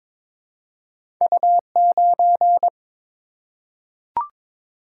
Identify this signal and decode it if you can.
Morse code, heard as a keyed tone, about 22 wpm, 700 Hz. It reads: U9
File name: practice / 22wpm / Two Letter-Number Combinations - 22wpm